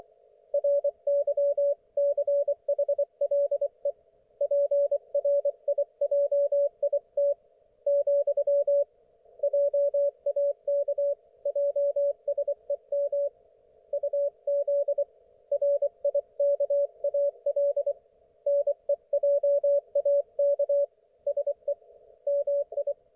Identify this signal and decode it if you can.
Morse code, heard as a keyed tone, about 24 wpm, 570 Hz. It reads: RYCHLE PRIJIT , JAK JSEM UZ RIKAL NEJAK SE MI
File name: Archiv / CUC224 / ok1fhi cul224